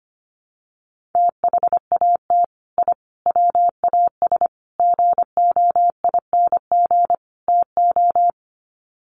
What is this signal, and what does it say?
Morse code, heard as a keyed tone, about 25 wpm, 700 Hz. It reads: THAT I WAS GOING TO
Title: Sets of 5 Words Encore 25wpm